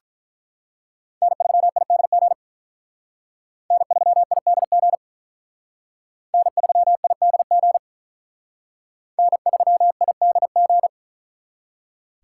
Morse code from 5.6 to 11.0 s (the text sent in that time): N3IDG N3IDG